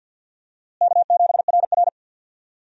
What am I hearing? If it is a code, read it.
Morse code, heard as a keyed tone, about 48 wpm, 700 Hz. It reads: K7RR
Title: USA Call Signs - 4 Characters - Repeated 2x - 50wpm